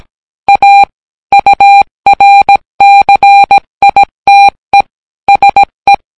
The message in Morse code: A URCITE SE